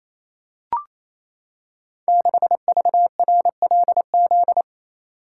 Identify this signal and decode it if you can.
Morse code, heard as a keyed tone, about 29 wpm, 700 Hz. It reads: 6VRLZ
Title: Code Groups - 5 Letter:Numbers - 28wpm